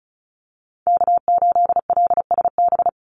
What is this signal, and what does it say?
Morse code, heard as a keyed tone, about 34 wpm, 700 Hz. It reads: K8LSB
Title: Call Signs Repeated 3x - 35wpm